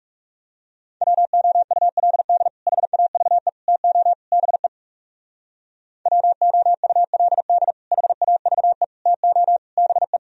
Morse code, heard as a keyed tone, about 42 wpm, 700 Hz. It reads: WOULD HAVE TO BE WOULD HAVE TO BE